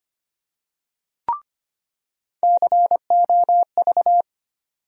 Morse code, heard as a keyed tone, about 25 wpm, 700 Hz. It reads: COV
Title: USA Call Sign Postfixes - 3 Letters 25wpm